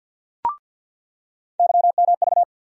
Morse code, heard as a keyed tone, about 54 wpm, 700 Hz. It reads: YMV
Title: USA Call Sign Postfixes - 3 Letters 50wpm